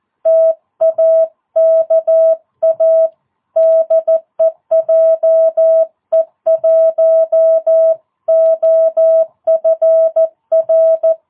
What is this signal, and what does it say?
Morse code, heard as a keyed tone, about 14 wpm, 650 Hz. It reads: TAKA DE JE1OFR
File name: jh8cbh / 2022cw